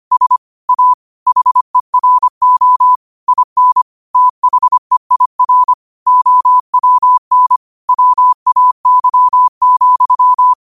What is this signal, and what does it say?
Morse code, heard as a keyed tone, about 25 wpm, 1000 Hz. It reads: S A HERO IN THEIR OWN WAY,
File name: qotdfiles / 25wpm